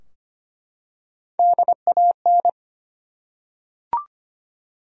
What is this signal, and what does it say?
Morse code, heard as a keyed tone, about 25 wpm, 700 Hz. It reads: DAN
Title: USA Common Names - Max 5 Letters - 25wpm